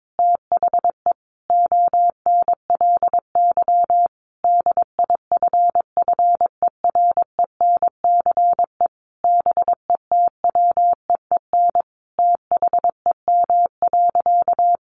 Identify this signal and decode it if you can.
Morse code, heard as a keyed tone, about 22 wpm, 700 Hz. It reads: THE ONLY DIFFERENCE BETWEEN THEM.
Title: Sentences from Top 800 Words 22wpm